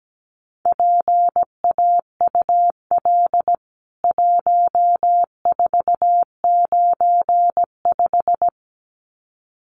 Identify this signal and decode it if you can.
Morse code, heard as a keyed tone, about 17 wpm, 700 Hz. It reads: PAUL 1495
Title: CWT Exchange 17wpm